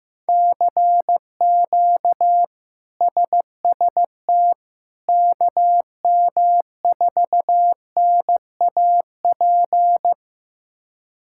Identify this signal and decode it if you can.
Morse code, heard as a keyed tone, about 15 wpm, 700 Hz. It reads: CQ SST KM4NAP
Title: K1USN SST QSO Practice - 15wpm